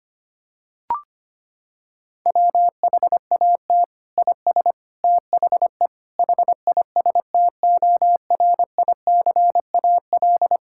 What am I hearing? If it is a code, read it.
Morse code, heard as a keyed tone, about 25 wpm, 700 Hz. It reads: WHAT IS THE HISTORICAL
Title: Sentences from Top 1000 Words - Review Part 5 - 25wpm